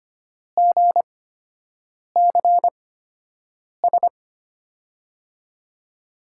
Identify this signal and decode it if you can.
Morse code, heard as a keyed tone, about 25 wpm, spaced as for 7 wpm, 700 Hz. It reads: GCS
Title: USA Call Signs - 8x Character Spacing - 25wpm